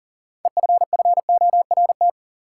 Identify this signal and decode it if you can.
Morse code, heard as a keyed tone, about 37 wpm, 700 Hz. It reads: EFFORT
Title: Sentences from Top 1200 Words - New Word Drill - 40wpm